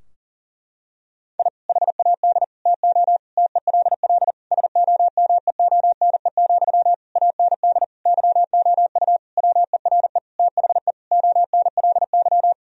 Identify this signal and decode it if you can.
Morse code, heard as a keyed tone, about 40 wpm, 700 Hz. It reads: I HAD TO TELL SOMEONE, AND YOU WERE THE ONLY
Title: Sentences from Top 1000 Words - Review Part 4 - 40wpm